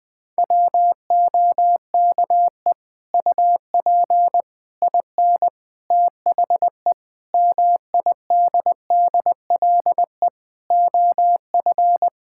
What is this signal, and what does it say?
Morse code, heard as a keyed tone, about 21 wpm, 700 Hz. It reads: WOKE UP IN THE MIDDLE OF
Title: Sets of 8 Words - 20wpm